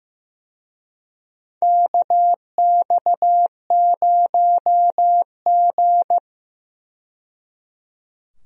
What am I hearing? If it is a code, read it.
Morse code, heard as a keyed tone, about 15 wpm, 700 Hz. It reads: KX0G
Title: USA Call Signs - 4 Characters - 15wpm